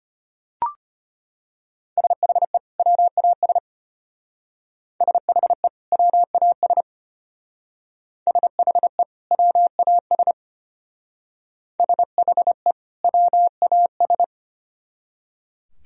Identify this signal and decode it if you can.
Morse code, heard as a keyed tone, about 32 wpm, 700 Hz. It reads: SHE WAS SHE WAS SHE WAS SHE WAS